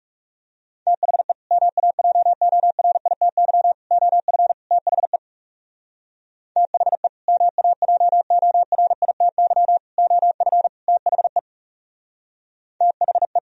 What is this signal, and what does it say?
Morse code, heard as a keyed tone, about 38 wpm, 700 Hz. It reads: THE MAJORITY OF THE THE MAJORITY OF THE THE